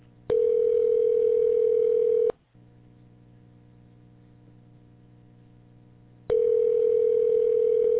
Ringback tone (North America), 440 + 480 Hz.